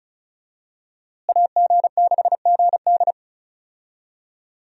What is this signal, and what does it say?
Morse code, heard as a keyed tone, about 34 wpm, 700 Hz. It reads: AG6GD
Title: USA Call Signs - 5 Characters - Repeated 2x - 35wpm